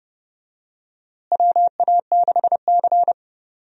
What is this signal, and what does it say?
Morse code, heard as a keyed tone, about 30 wpm, 700 Hz. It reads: WA6C
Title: USA Call Signs - 4 Characters - Repeated 3x - 30wpm